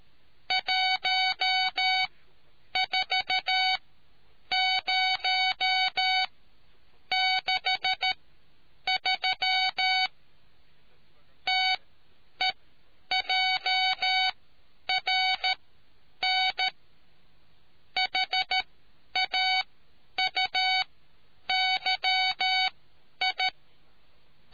Morse code, heard as a keyed tone, about 13 wpm, spaced as for 9 wpm, 2250 Hz. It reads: 14063 TEJRN HAUYI